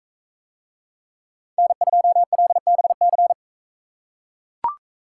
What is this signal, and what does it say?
Morse code, heard as a keyed tone, about 40 wpm, 700 Hz. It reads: N2LBC